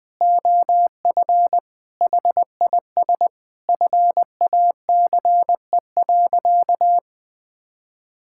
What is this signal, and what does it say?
Morse code, heard as a keyed tone, about 20 wpm, 700 Hz. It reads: OF HIS FACE.
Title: Sentences from Top 1000 Words - Review Part 5 - 20wpm